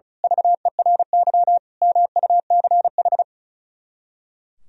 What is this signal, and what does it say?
Morse code, heard as a keyed tone, about 35 wpm, 700 Hz. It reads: VERY MUCH
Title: Speed-Racing - Sets of 5 Words 35wpm